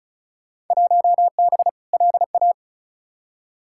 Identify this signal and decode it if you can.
Morse code, heard as a keyed tone, about 36 wpm, 700 Hz. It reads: 1B LA
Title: Field Day Exchanges Encore - 35wpm